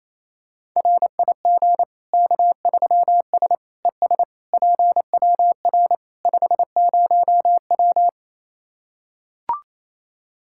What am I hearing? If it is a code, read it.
Morse code, heard as a keyed tone, about 28 wpm, 700 Hz. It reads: RIG K3S ES PWR 50W
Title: K1USN SST QSO Practice 28wpm